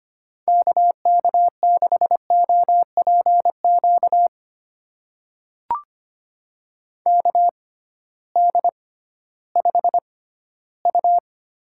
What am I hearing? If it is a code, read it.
Morse code, heard as a keyed tone, about 25 wpm, spaced as for 15 wpm, 700 Hz. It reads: KK6OPQ KD5U
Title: USA Call Signs - 6 Characters - 6x Character Spacing - 25wpm